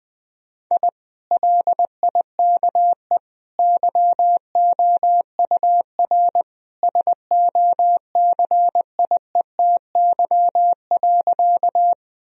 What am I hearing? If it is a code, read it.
Morse code, heard as a keyed tone, about 20 wpm, 700 Hz. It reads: I LIKE YOUR SOCIETY.
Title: Sentences from Top 1000 Words - Review Part 4 - 20wpm